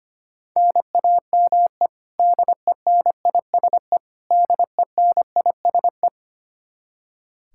Morse code, heard as a keyed tone, about 25 wpm, 700 Hz. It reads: NAME DENISE DENISE
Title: Practice QSO - Exchange 1 - Beginning - 25wpm